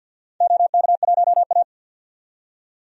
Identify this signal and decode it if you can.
Morse code, heard as a keyed tone, about 51 wpm, 700 Hz. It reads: OK1A
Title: DXCC-100 Call Signs - 4 Characters - 50wpm